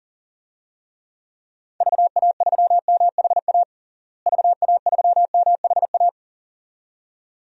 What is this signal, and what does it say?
Morse code, heard as a keyed tone, about 40 wpm, 700 Hz. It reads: VA3MHA VA3MHA
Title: DXCC-100 Call Signs - 6 Characters - Repeated 2x - 40wpm